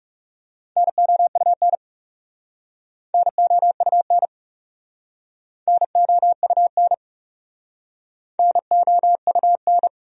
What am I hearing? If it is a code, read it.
Morse code, heard as a keyed tone, about 40 wpm, 700 Hz. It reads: NOUN NOUN NOUN NOUN